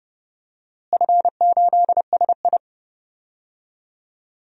Morse code, heard as a keyed tone, about 30 wpm, 700 Hz. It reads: F8SI
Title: DXCC-100 Call Signs - 4 Characters - Repeated 3x - 30wpm